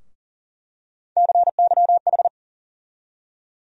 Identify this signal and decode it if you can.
Morse code, heard as a keyed tone, about 37 wpm, 700 Hz. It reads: CYH
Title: USA Call Sign Postfixes - 3 Letters 40wpm